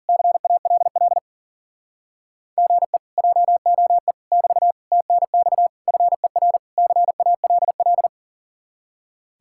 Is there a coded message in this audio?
Yes: Morse code from 0.1 to 8.1 s, heard as a keyed tone, about 51 wpm, 700 Hz. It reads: CALL GE JOE <BT> TNX FER CALL